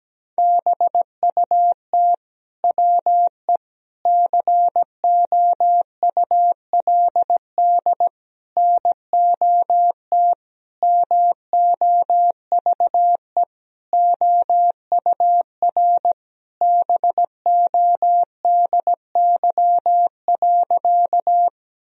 Morse code, heard as a keyed tone, about 17 wpm, 700 Hz. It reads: BUT WE COULD NOT MOVE OUR BODY.
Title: Sentences from Top 1000 Words - Review Part 2 - 17wpm